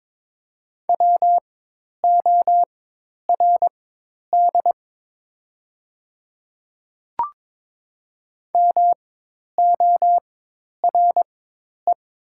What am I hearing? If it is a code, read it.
Morse code, heard as a keyed tone, about 22 wpm, spaced as for 10 wpm, 700 Hz. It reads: WORD MORE
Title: 4x Character Spacing - Top 100 Words - 22wpm